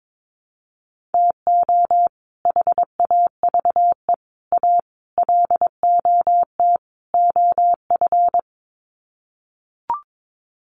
Morse code, heard as a keyed tone, about 22 wpm, 700 Hz. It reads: TO HAVE A LOT OF
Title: Sets of 5 Words Encore 22wpm